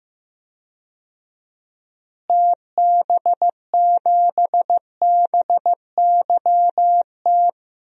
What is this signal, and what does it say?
Morse code, heard as a keyed tone, about 15 wpm, 700 Hz. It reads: TB7BYT
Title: DXCC-100 Call Signs - 6 Characters - Repeated 3x - 15wpm